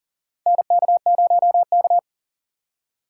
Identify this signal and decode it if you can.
Morse code, heard as a keyed tone, about 40 wpm, 700 Hz. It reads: NK0K
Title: USA Call Signs - 6x Character Spacing - 40wpm